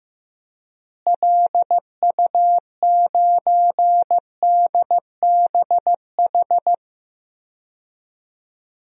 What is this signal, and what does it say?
Morse code, heard as a keyed tone, about 15 wpm, 700 Hz. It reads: LU9DBH
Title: DXCC-100 Call Signs - 6 Characters - Repeated 2x - 15wpm